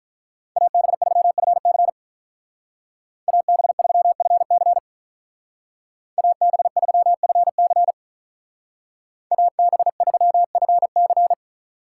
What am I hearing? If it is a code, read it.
Morse code, heard as a keyed tone, about 47 wpm, 700 Hz. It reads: AB3FC AB3FC AB3FC AB3FC